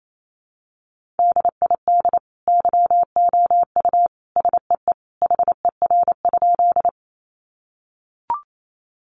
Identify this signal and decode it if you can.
Morse code, heard as a keyed tone, about 28 wpm, 700 Hz. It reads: DID YOU SEE HER?
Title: Sentences from Top 100 Words 28wpm